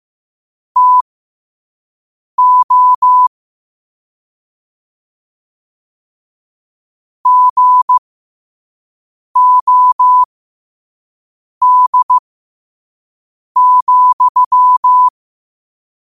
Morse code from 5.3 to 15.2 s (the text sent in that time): GOD,